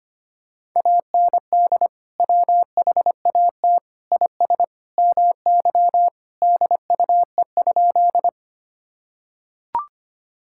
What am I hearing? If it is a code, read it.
Morse code, heard as a keyed tone, about 25 wpm, 700 Hz. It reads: AND WHAT IS MY DUE?
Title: Sentences from Top 1000 Words - Review Part 6 - 25wpm